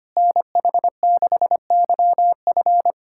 Morse code, heard as a keyed tone, about 25 wpm, 700 Hz. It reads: NH6YF
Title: USA Callsign DE Callsign - 5 Characters - 25wpm